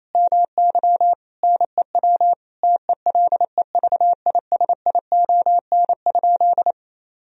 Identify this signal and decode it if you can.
Morse code, heard as a keyed tone, about 28 wpm, 700 Hz. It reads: MY NEW TELEVISION?
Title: Sentences from Top 1000 Words - Review Part 4 - 28wpm